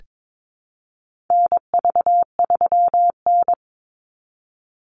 Morse code, heard as a keyed tone, about 22 wpm, 690 Hz. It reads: NV3N
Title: Call Signs Repeated 2x - 22wpm